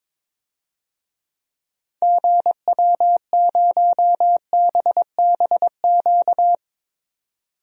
Morse code, heard as a keyed tone, about 22 wpm, 700 Hz. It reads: GW0BBQ
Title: DXCC-100 Call Signs - 6 Characters - 22wpm